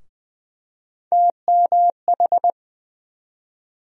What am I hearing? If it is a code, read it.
Morse code, heard as a keyed tone, about 20 wpm, 700 Hz. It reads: TMH
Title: USA Call Sign Postfixes - 3 Letters 20wpm